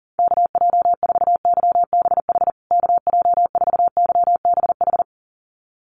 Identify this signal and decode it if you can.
Morse code, heard as a keyed tone, about 37 wpm, 700 Hz. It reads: KJ4YBH KJ4YBH
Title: USA Call Signs Repeated 2x 40wpm